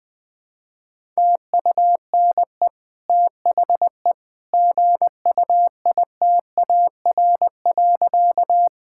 Morse code, heard as a keyed tone, about 20 wpm, 700 Hz. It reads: TUNE THE GUITAR.
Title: Easy Sentences with Hints - 20wpm